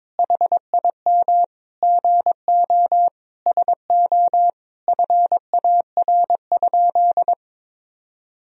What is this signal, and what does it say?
Morse code, heard as a keyed tone, about 22 wpm, 700 Hz. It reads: HIM GO SO FAR?
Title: Sentences from Top 300 Words and Abbreviations 22wpm